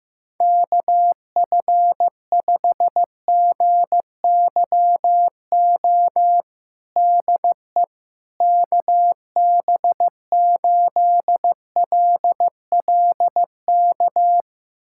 Morse code, heard as a keyed tone, about 15 wpm, 700 Hz. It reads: KF5GYO DE KB8LLK
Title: USA Callsign DE Callsign 15wpm